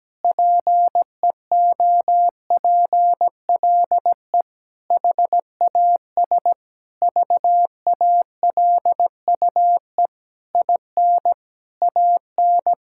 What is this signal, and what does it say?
Morse code, heard as a keyed tone, about 17 wpm, 700 Hz. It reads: PEOPLE HAS VALUE IN AN
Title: Sentences from Top 1000 Words - Review Part 6 - 17wpm